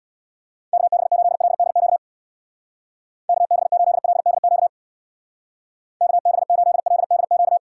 Morse code, heard as a keyed tone, about 60 wpm, 700 Hz. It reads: KB8LDQ KB8LDQ KB8LDQ
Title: Speed Racing - USA Call Signs 50wpm